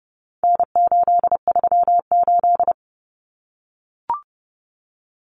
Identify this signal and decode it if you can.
Morse code, heard as a keyed tone, about 30 wpm, 700 Hz. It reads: N838
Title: Common Cut Numbers - 4-Digit 30wpm